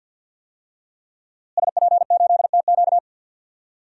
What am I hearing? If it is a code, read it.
Morse code, heard as a keyed tone, about 48 wpm, 700 Hz. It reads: SP8TQ
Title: DXCC-100 Call Signs - 5 Characters - Repeated 2x - 50wpm